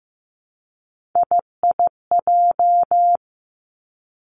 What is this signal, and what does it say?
Morse code, heard as a keyed tone, about 15 wpm, 700 Hz. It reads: IIJ
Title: Three Challenging Letter-Number Combinations 15wpm